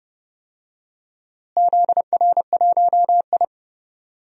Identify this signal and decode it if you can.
Morse code, heard as a keyed tone, about 30 wpm, 700 Hz. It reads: ZR1I
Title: DXCC-100 Call Signs - 4 Characters - Repeated 3x - 30wpm